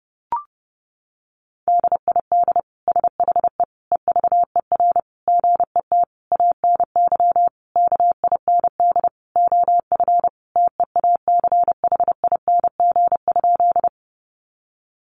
Morse code, heard as a keyed tone, about 30 wpm, 700 Hz. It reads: DID SHE EVER GET ANY KIND OF TEACHING?
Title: Sentences from Top 1100 Words 30wpm